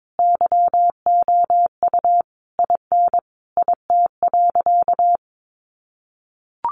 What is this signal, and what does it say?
Morse code, heard as a keyed tone, about 22 wpm, 700 Hz. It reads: YOU IN IT.
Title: Sentences from Top 1500 Words 22wpm